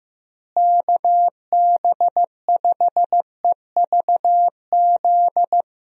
Morse code, heard as a keyed tone, about 15 wpm, 700 Hz. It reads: KB5EVZ
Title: USA Call Signs - 6 Characters - Repeated 2x - 15wpm